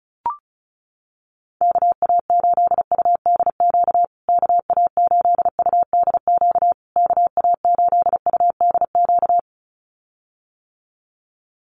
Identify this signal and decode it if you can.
Morse code, heard as a keyed tone, about 34 wpm, 700 Hz. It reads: KA8UDQ KA8UDQ KA8UDQ
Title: USA Call Signs Repeated 3x 35wpm